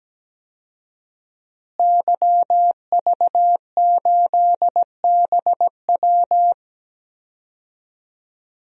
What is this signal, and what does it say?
Morse code, heard as a keyed tone, about 17 wpm, 700 Hz. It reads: YV8BW
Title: DXCC-100 Call Signs - 5 Characters - Repeated 2x - 17wpm